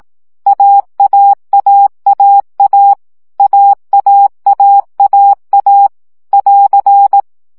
Morse code, heard as a keyed tone, about 18 wpm, 800 Hz. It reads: AAAAA AAAAA <AR>
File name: CW gyakorlas a 15 baud